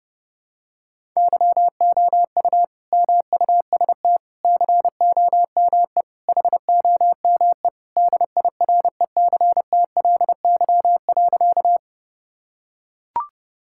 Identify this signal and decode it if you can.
Morse code, heard as a keyed tone, about 30 wpm, 700 Hz. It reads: YOU MUST COME HOME DIRECTLY.
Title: Sentences from Top 1000 Words - Review Part 5 - 30wpm